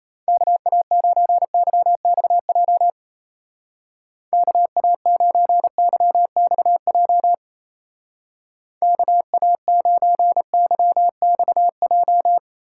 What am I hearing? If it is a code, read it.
Morse code, heard as a keyed tone, about 34 wpm, 700 Hz. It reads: KA9YXJ KA9YXJ KA9YXJ
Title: Speed Racing - USA 2x3 Call Signs 28wpm